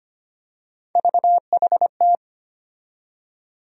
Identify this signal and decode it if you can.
Morse code, heard as a keyed tone, about 25 wpm, 700 Hz. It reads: VHT
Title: USA Call Sign Postfixes - 3 Letters 25wpm